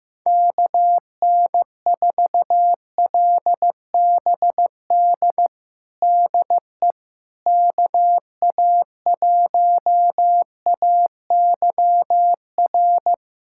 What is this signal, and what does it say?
Morse code, heard as a keyed tone, about 15 wpm, 700 Hz. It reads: KN4LBD DE KA1AYR
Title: USA Callsign DE Callsign - 6 Characters - 15wpm